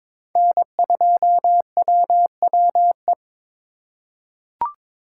Morse code, heard as a keyed tone, about 22 wpm, 700 Hz. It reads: N2WWE
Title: USA Call Signs - 2x Character Spacing - 22wpm